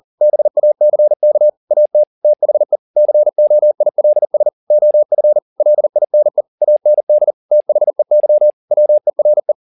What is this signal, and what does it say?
Morse code, heard as a keyed tone, about 40 wpm, 600 Hz. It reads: BACK AT THE COILS OF LINE AND THEY WERE
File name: the old man and the sea 2 40wpm